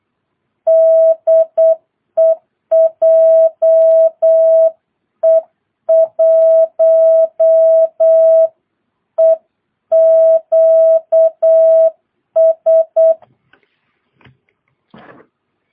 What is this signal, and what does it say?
Morse code, heard as a keyed tone, about 8 wpm, 650 Hz. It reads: DE JE1EQS
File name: jh8cbh / 2022cw